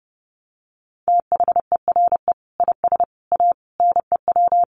Morse code, heard as a keyed tone, about 30 wpm, 700 Hz. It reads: THERE IS A NEW
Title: Sentences from Top 1100 Words 30wpm